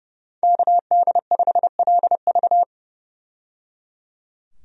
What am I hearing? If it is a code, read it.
Morse code, heard as a keyed tone, about 30 wpm, 700 Hz. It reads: KD5LV
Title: USA Call Signs - 5 Characters - 30wpm